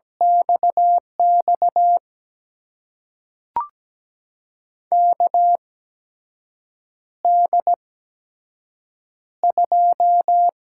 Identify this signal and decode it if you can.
Morse code, heard as a keyed tone, about 17 wpm, spaced as for 6 wpm, 700 Hz. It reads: XX KD2
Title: USA Call Signs - 6 Characters - 8x Character Spacing - 17wpm